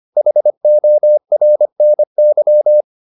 Morse code, heard as a keyed tone, about 25 wpm, 600 Hz. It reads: HORNY
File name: the old man and the sea part 1 25wpm